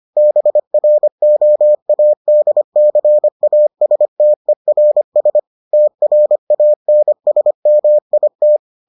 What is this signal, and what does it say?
Morse code, heard as a keyed tone, about 25 wpm, 600 Hz. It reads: BROADCASTERS TRANSMIT